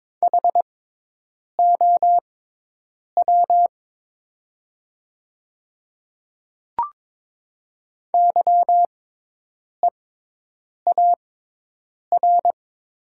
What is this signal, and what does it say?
Morse code, heard as a keyed tone, about 22 wpm, spaced as for 8 wpm, 700 Hz. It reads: HOW YEAR
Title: 6x Character Spacing - Top 200 Words - 22wpm